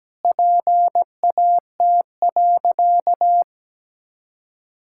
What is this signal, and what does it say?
Morse code, heard as a keyed tone, about 17 wpm, 700 Hz. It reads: PAT.
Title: Pangram Challenge v2 - 17wpm